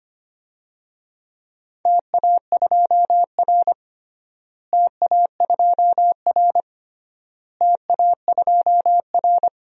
Morse code, heard as a keyed tone, about 25 wpm, 700 Hz. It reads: TA2R TA2R TA2R
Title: DXCC-100 Call Signs - 4 Characters - Repeated 3x - 25wpm